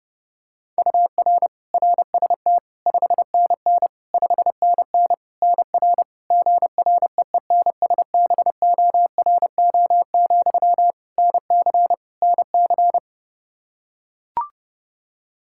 Morse code, heard as a keyed tone, about 32 wpm, 700 Hz. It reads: UR RST 5NN 5NN NR GREENSBORO, NC NC
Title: Practice QSO - Exchange 1 - Beginning - 30wpm